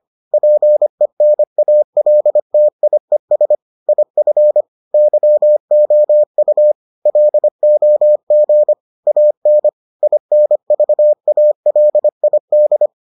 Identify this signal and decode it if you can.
Morse code, heard as a keyed tone, about 25 wpm, 600 Hz. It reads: PENALTIES IF YOU LOG AN INVALID